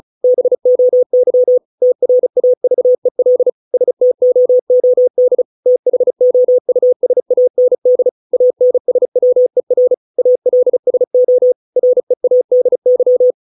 Morse code, heard as a keyed tone, about 34 wpm, 500 Hz. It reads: BOY TRAVEL STOOD THOUSAND ANSWER ALSO READY